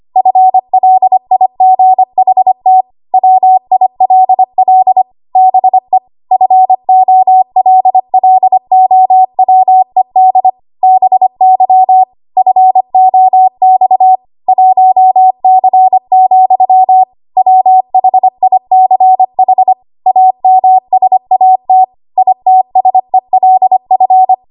Morse code, heard as a keyed tone, about 25 wpm, 750 Hz. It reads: FLIGHT WILL BE FOLLOWED BY FOX 1C, WHICH AMSAT ITSELF
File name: Archive / 25 WPM / 25WPM